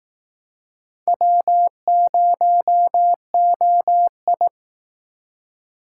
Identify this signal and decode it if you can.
Morse code, heard as a keyed tone, about 19 wpm, 700 Hz. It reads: W0OI